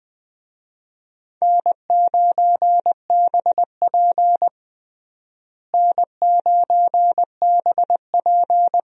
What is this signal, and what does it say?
Morse code, heard as a keyed tone, about 20 wpm, 700 Hz. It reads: N9BP N9BP
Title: USA Call Signs - 4 Characters - Repeated 2x - 20wpm